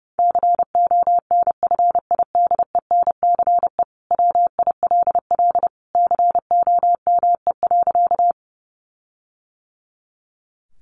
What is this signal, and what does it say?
Morse code, heard as a keyed tone, about 30 wpm, 700 Hz. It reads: CONFIDENCE WILL COME.
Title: Sentences from Top 1500 Words 30wpm